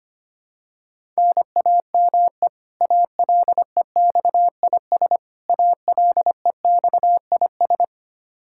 Morse code, heard as a keyed tone, about 25 wpm, 700 Hz. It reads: NAME ALEXIS ALEXIS
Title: Practice QSO - Exchange 1 - Beginning - 25wpm